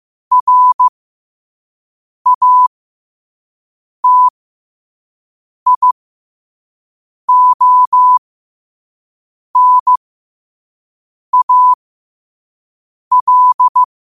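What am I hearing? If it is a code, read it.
Morse code, heard as a keyed tone, about 15 wpm, spaced as for 5 wpm, 1000 Hz. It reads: RATIONAL
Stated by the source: Morse code quote of the day at 5 wpm.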